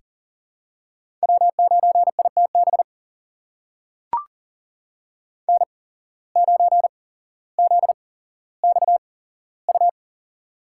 Morse code, heard as a keyed tone, about 42 wpm, spaced as for 17 wpm, 700 Hz. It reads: W9ITB N9ZXU
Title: USA Call Signs - 8x Character Spacing - 40wpm